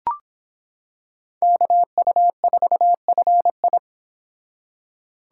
Morse code, heard as a keyed tone, about 26 wpm, 700 Hz. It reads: KU4FI